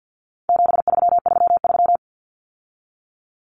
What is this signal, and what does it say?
Morse code, heard as a keyed tone, about 50 wpm, 700 Hz. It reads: 7333
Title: Numbers - 4-Digit 50wpm